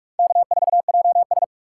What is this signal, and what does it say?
Morse code, heard as a keyed tone, about 46 wpm, 700 Hz. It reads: K4JS
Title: USA Call Signs - 4 Characters - 8x Character Spacing - 45wpm